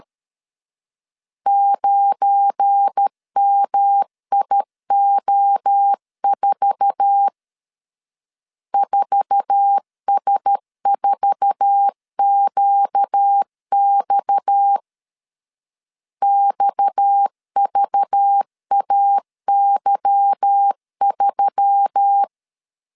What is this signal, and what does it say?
Morse code, heard as a keyed tone, about 13 wpm, 800 Hz. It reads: 9MIO4 4S4QX XVAY3
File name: LEZIONI / 8 YJ45 / 8 YJ4 12wpm 2151